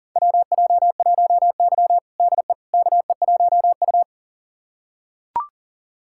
Morse code, heard as a keyed tone, about 40 wpm, 700 Hz. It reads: WJ1Y DE KE1U
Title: USA Callsign DE Callsign - 4 Characters - 40wpm